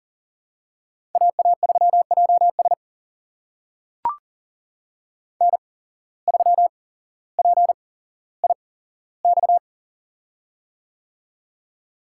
Morse code, heard as a keyed tone, about 39 wpm, spaced as for 17 wpm, 700 Hz. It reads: AA3JS N3PIX
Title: USA Call Signs - 5 Characters - 8x Character Spacing - 40wpm